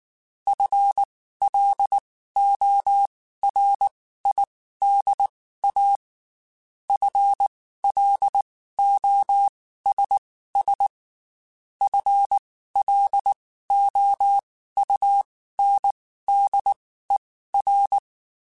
Morse code, heard as a keyed tone, about 19 wpm, spaced as for 14 wpm, 800 Hz. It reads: FLORIDA FLOSS FLOUNDER
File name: Spaced / FL Runs sp mp3s / FL Runs sp @18